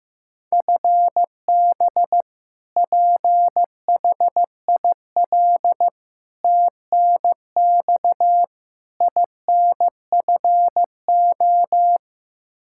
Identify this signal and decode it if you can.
Morse code, heard as a keyed tone, about 15 wpm, 700 Hz. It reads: FB PHIL TNX INFO
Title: Practice QSO - Exchange 3 - More Info - 15wpm